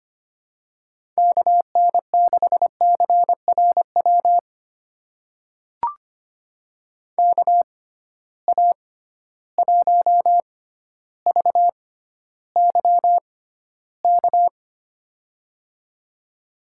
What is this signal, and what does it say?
Morse code, heard as a keyed tone, about 25 wpm, spaced as for 13 wpm, 700 Hz. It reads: KN6CRW KA1VYK
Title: USA Call Signs - 6 Characters - 6x Character Spacing - 25wpm